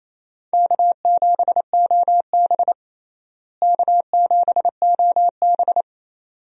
Morse code, heard as a keyed tone, about 28 wpm, 700 Hz. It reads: K7OB K7OB
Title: USA Call Signs - 4 Characters - Repeated 2x - 28wpm